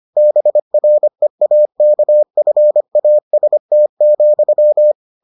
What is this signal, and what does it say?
Morse code, heard as a keyed tone, about 25 wpm, 600 Hz. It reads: BREAKFAST,